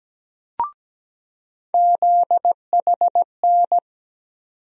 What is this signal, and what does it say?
Morse code, heard as a keyed tone, about 17 wpm, 700 Hz. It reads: ZHN
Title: USA Call Sign Postfixes - 3 Letters 17wpm